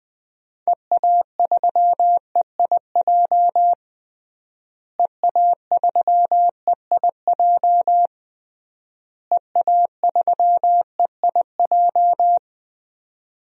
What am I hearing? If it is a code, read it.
Morse code, heard as a keyed tone, about 20 wpm, 700 Hz. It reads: EA3EIJ EA3EIJ EA3EIJ
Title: DXCC-100 Call Signs - 6 Characters - Repeated 3x - 20wpm